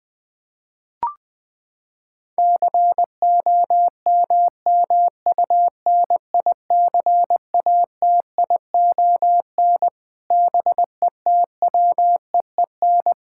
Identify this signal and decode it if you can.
Morse code, heard as a keyed tone, about 20 wpm, 700 Hz. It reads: COMMUNICATION BETWEEN
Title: Sentences from Top 1000 Words - Review Part 6 - 20wpm